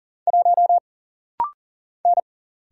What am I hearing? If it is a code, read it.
Morse code, heard as a keyed tone, about 40 wpm, 700 Hz. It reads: 1 N
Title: Single Letter-Number Alt - Rapid-Fire 40wpm